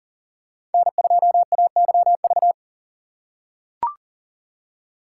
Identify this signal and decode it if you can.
Morse code, heard as a keyed tone, about 39 wpm, 700 Hz. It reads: N2AYV
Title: USA Call Signs - 5 Characters - 3x Character Spacing - 40wpm